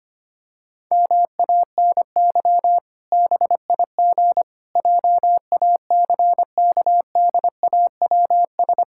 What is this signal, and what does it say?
Morse code, heard as a keyed tone, about 25 wpm, 700 Hz. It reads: MANY BIG JACKDAWS
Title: Pangram Challenge v2 - 25wpm